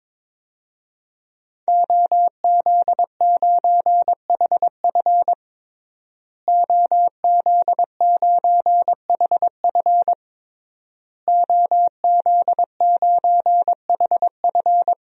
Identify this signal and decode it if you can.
Morse code, heard as a keyed tone, about 22 wpm, 700 Hz. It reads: OZ9HF OZ9HF OZ9HF
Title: DXCC-100 Call Signs - 5 Characters - Repeated 3x - 22wpm